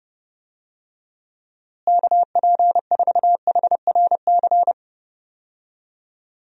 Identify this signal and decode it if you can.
Morse code, heard as a keyed tone, about 30 wpm, 700 Hz. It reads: KP4HRC
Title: USA Call Signs - 6 Characters - Repeated 2x - 30wpm